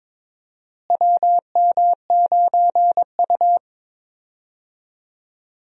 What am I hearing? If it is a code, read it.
Morse code, heard as a keyed tone, about 22 wpm, 700 Hz. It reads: WM9U
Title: USA Call Signs - 4 Characters - Repeated 2x - 22wpm